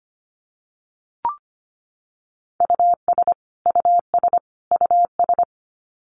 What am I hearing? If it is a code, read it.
Morse code, heard as a keyed tone, about 25 wpm, 700 Hz. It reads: US US US
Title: Top 300 Words Repeated 25wpm